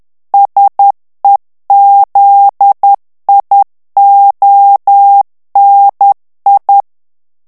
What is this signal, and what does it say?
Morse code, heard as a keyed tone, about 10 wpm, 800 Hz. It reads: SEZIONI
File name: ITALIEN texte clair 2 vitesse 10 mots minute